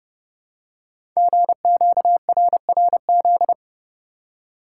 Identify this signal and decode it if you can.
Morse code, heard as a keyed tone, about 30 wpm, 700 Hz. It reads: GQRRZ
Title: Code Groups - 5 Letters - 30wpm